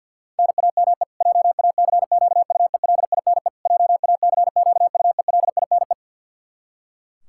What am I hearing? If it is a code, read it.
Morse code, heard as a keyed tone, about 48 wpm, 700 Hz. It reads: NAME JACQUELINE JACQUELINE
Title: Practice QSO - Exchange 1 - Beginning - 50wpm